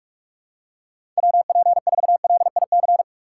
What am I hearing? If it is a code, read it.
Morse code, heard as a keyed tone, about 41 wpm, 700 Hz. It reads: WP4LIC